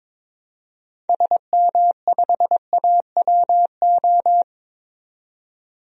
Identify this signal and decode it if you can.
Morse code, heard as a keyed tone, about 22 wpm, 700 Hz. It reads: SM5AWO
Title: DXCC-100 Call Signs - 6 Characters - 22wpm